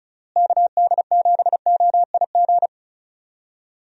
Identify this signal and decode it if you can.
Morse code, heard as a keyed tone, about 34 wpm, 700 Hz. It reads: KD7OIG
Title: USA Call Signs - 6 Characters - Repeated 3x - 35wpm